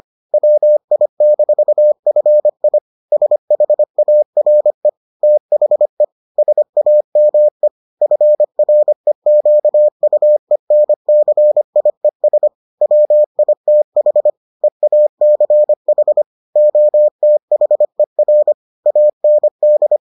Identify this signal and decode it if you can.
Morse code, heard as a keyed tone, about 25 wpm, 600 Hz. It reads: WI-FI SHARE THE SAME FREQUENCIES WITH EACH OTHER AND